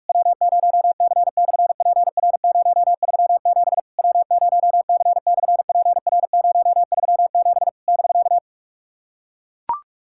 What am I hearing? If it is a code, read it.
Morse code, heard as a keyed tone, about 44 wpm, 690 Hz. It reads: W0C/PR037 W0C/PR037 <BK>
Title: SOTA QSO Practice 45wpm